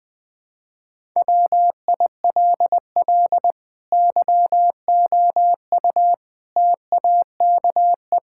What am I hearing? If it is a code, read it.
Morse code, heard as a keyed tone, about 21 wpm, 700 Hz. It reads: WILL YOU TAKE
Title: Sentences from Top 1000 Words - Review Part 6 - 20wpm